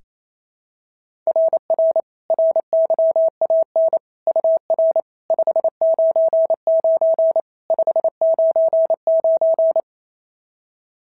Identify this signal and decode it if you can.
Morse code, heard as a keyed tone, about 28 wpm, 660 Hz. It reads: RR RYAN UR 599 599
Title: SOTA QSO Practice 28wpm